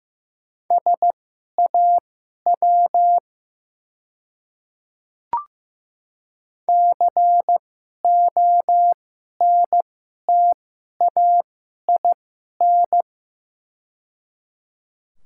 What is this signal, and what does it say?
Morse code, heard as a keyed tone, about 15 wpm, spaced as for 11 wpm, 690 Hz. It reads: SAW CONTAIN
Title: 2x Character Spacing - Top 500 Words - 15wpm